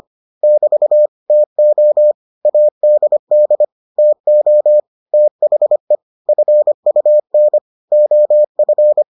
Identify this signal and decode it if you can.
Morse code, heard as a keyed tone, about 25 wpm, 600 Hz. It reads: <BT> TO ADD TO THE FUN OF